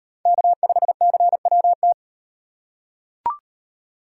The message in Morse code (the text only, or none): K5CWT